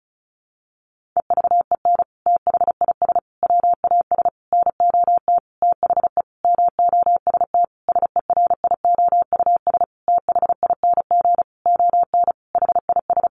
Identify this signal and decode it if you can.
Morse code, heard as a keyed tone, about 34 wpm, 700 Hz. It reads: EVEN THIS WAS NOT THE MOST SERIOUS THING ON HIS
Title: Sentences from Top 1100 Words 35wpm